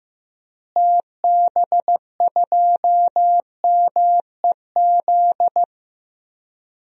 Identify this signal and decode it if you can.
Morse code, heard as a keyed tone, about 15 wpm, 700 Hz. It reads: TB2MEZ
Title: DXCC-100 Call Signs - 6 Characters - Repeated 3x - 15wpm